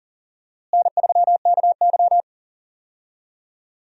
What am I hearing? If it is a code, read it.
Morse code, heard as a keyed tone, about 40 wpm, 700 Hz. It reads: N3KY
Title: USA Call Signs - 4 Characters - Repeated 2x - 40wpm